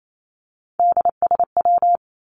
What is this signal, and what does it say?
Morse code, heard as a keyed tone, about 28 wpm, 700 Hz. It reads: DSW
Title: Abbreviations Repeated 3x - 28wpm